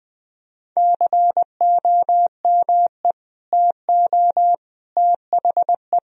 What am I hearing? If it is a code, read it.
Morse code, heard as a keyed tone, about 19 wpm, 700 Hz. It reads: COME TO THE
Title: Sentences from Top 1000 Words - Review Part 2 - 20wpm